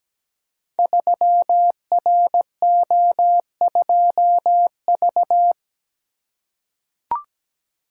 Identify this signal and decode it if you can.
Morse code, heard as a keyed tone, about 17 wpm, 700 Hz. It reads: 3RO2V
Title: Code Groups - 5 Letter:Numbers - 17wpm